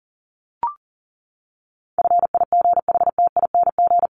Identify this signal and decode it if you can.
Morse code, heard as a keyed tone, about 37 wpm, 700 Hz. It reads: FIGHTING
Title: ING Suffix - Top 100 Words with Base Words 40wpm